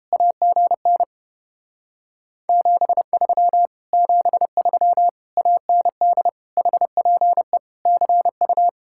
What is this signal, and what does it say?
Morse code, heard as a keyed tone, about 29 wpm, 700 Hz. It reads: AGN 73 73 AND HPE CU